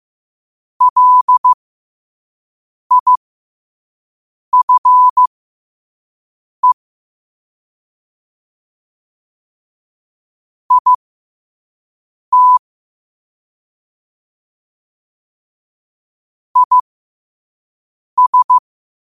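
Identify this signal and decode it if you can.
Morse code, heard as a keyed tone, about 15 wpm, spaced as for 5 wpm, 1000 Hz. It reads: LIFE IT IS
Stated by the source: Morse code sent at 5 wpm.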